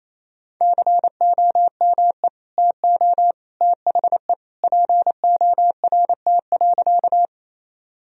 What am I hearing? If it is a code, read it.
Morse code, heard as a keyed tone, about 28 wpm, 700 Hz. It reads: COME TO THE PORT.
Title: Sentences from Top 1000 Words - Review Part 1 - 28wpm